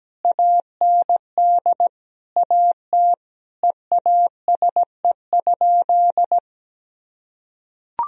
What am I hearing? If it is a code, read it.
Morse code, heard as a keyed tone, about 17 wpm, 700 Hz. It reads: AND AT EASE?
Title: Sentences from Top 1000 Words - Review Part 1 - 17wpm